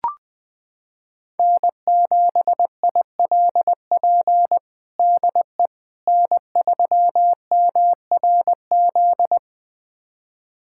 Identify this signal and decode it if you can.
Morse code, heard as a keyed tone, about 20 wpm, 700 Hz. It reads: N7ILP DE N3MRZ
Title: USA Callsign DE Callsign - 5 Characters - 20wpm